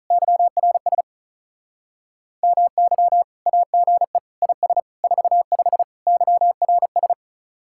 Morse code, heard as a keyed tone, about 43 wpm, 700 Hz. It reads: YRS MY AGE IS 45 YRS
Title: Speed-Racing Common QSO Phrases - 35wpm